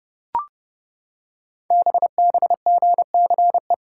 Morse code, heard as a keyed tone, about 28 wpm, 700 Hz. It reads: BBGCE
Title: Code Groups - 5 Letter:Numbers - 30wpm